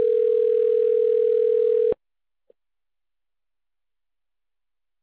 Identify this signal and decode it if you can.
Ringback tone (North America), 440 + 480 Hz.